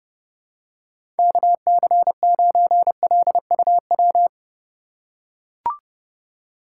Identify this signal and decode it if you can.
Morse code, heard as a keyed tone, about 30 wpm, 700 Hz. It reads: KC9LUW
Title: USA Call Signs - 6 Characters - 2x Character Spacing - 30wpm